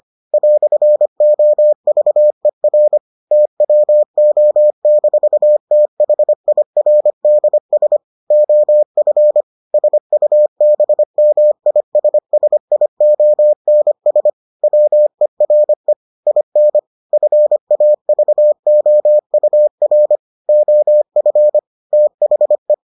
Morse code, heard as a keyed tone, about 25 wpm, 600 Hz. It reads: "OVER TWO-THIRDS OF SUBMISSIONS WERE IN FAVOUR OF THE